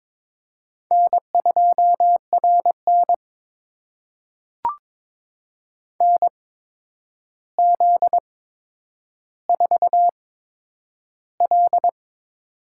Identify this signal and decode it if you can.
Morse code, heard as a keyed tone, about 22 wpm, spaced as for 9 wpm, 700 Hz. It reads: N2RN NZ4L
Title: USA Call Signs - 8x Character Spacing - 22wpm